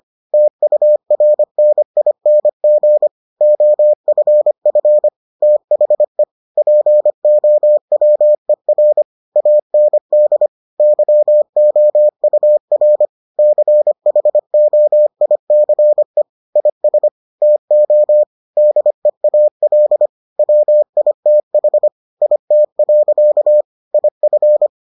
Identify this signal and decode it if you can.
Morse code, heard as a keyed tone, about 25 wpm, 600 Hz. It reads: TURNING OFF THE POWER AND YOUR CHOICE IS TO DEAL WITH IT. IF